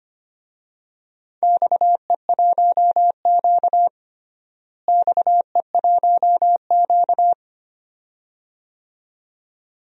Morse code, heard as a keyed tone, about 25 wpm, 700 Hz. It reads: XE1Q XE1Q
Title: DXCC-100 Call Signs - 4 Characters - Repeated 2x - 25wpm